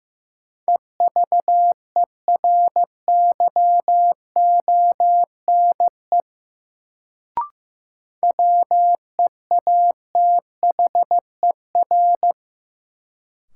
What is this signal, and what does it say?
Morse code, heard as a keyed tone, about 15 wpm, 700 Hz. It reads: EVERYONE WEATHER
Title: Sentences from Top 800 Words - New Word Drill - 15wpm